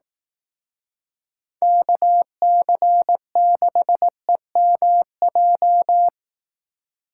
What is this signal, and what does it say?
Morse code, heard as a keyed tone, about 17 wpm, 700 Hz. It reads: KC6EMJ
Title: Speed Racing - USA 2x3 Call Signs 15wpm